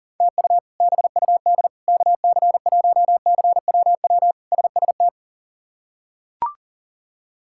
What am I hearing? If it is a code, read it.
Morse code, heard as a keyed tone, about 40 wpm, 700 Hz. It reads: TU BUD KC1CWW SST
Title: K1USN SST QSO Practice - 40wpm